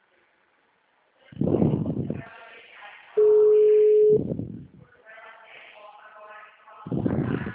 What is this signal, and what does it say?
Ringback tone (Europe, 425 Hz).